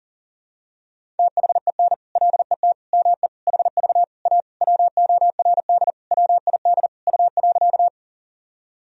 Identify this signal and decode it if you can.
Morse code, heard as a keyed tone, about 40 wpm, 700 Hz. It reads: THEN LET ME HV A WORD WID U.
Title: Sentences from Top 300 Words and Abbreviations 40wpm